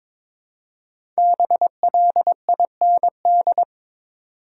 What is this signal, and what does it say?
Morse code, heard as a keyed tone, about 21 wpm, 700 Hz. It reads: BLIND
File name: 5 Letter Words - First 3 Letters Given - 22wpm